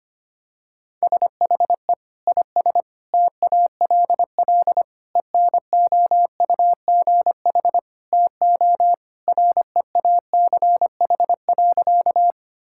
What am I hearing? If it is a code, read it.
Morse code, heard as a keyed tone, about 25 wpm, 700 Hz. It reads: SHE IS TALL ENOUGH TO REACH.
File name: practice / 25wpm / Sentences with 2-Word Phrases - 25wpm